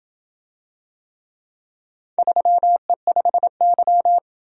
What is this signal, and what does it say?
Morse code, heard as a keyed tone, about 28 wpm, 700 Hz. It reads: 3E5Y